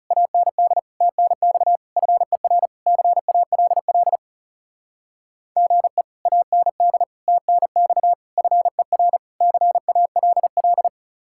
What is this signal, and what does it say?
Morse code, heard as a keyed tone, about 36 wpm, 700 Hz. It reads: AND TNX FER CALL GE AND TNX FER CALL
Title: Speed-Racing Common QSO Phrases Encore - 30wpm